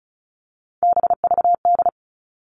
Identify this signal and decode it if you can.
Morse code, heard as a keyed tone, about 36 wpm, 700 Hz. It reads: BVD
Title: Three Challenging Letter-Number Combinations 35wpm